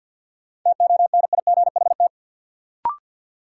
Morse code, heard as a keyed tone, about 51 wpm, 700 Hz. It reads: TONIGHT
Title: Sentences from Top 600 Words - New Word Drill - 50wpm